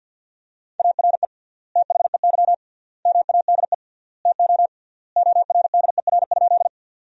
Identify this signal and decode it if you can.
Morse code, heard as a keyed tone, about 50 wpm, 700 Hz. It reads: ARE THEY MADE TO ORDER?
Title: 3x Word-Spacing - Sentences from Top 400 Words - 50wpm